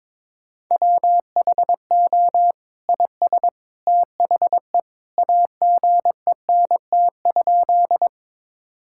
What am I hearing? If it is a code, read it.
Morse code, heard as a keyed tone, about 22 wpm, 700 Hz. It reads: WHO IS THE AGENT?
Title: Sentences from Top 1600 Words 22wpm